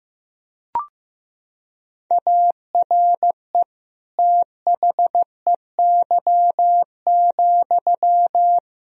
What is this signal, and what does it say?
Morse code, heard as a keyed tone, about 15 wpm, 700 Hz. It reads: ARE THEY,
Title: Sentences from Top 1000 Words - Review Part 2 - 15wpm